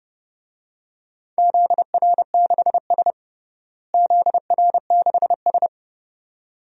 Morse code, heard as a keyed tone, about 31 wpm, 700 Hz. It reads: ZR6S ZR6S
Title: DXCC-100 Call Signs - 4 Characters - Repeated 2x - 30wpm